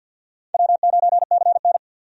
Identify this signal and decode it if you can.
Morse code, heard as a keyed tone, about 45 wpm, 700 Hz. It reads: W9CN
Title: USA Call Signs - 4 Characters - 50wpm